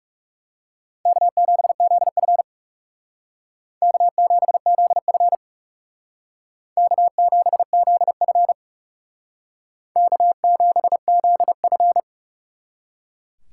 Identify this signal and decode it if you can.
Morse code, heard as a keyed tone, about 35 wpm, 700 Hz. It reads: K7ZF K7ZF K7ZF K7ZF
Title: Speed Racing - USA 1x2 Call Signs 30wpm